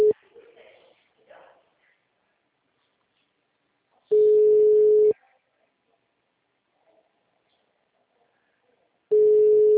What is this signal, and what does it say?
Ringback tone (Europe, 425 Hz), with speech.